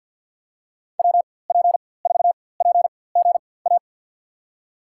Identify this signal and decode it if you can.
Morse code, heard as a keyed tone, about 48 wpm, spaced as for 23 wpm, 700 Hz. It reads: WP4PGA
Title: USA Call Signs - 6 Characters - 4x Character Spacing - 50wpm